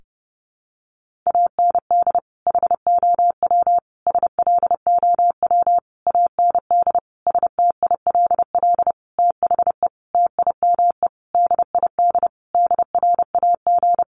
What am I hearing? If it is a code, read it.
Morse code, heard as a keyed tone, about 30 wpm, 700 Hz. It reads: AND HOW SLOW AND STILL THE TIME DID DRAG
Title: Sentences from Top 1500 Words 30wpm